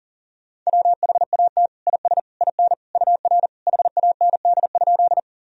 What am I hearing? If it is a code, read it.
Morse code, heard as a keyed tone, about 40 wpm, 700 Hz. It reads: WHAT IS IN UR HAND?
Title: Sentences from Top 200 Words and Abbreviations 40wpm